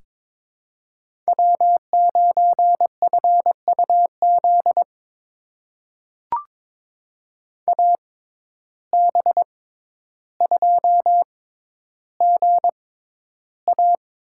Morse code, heard as a keyed tone, about 22 wpm, spaced as for 11 wpm, 700 Hz. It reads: W9FUZ AB2GA
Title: USA Call Signs - 5 Characters - 6x Character Spacing - 22wpm